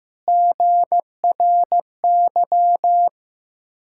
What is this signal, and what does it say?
Morse code, heard as a keyed tone, about 15 wpm, 690 Hz. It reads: GRY
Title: Ontario QSO Party Exchange 15wpm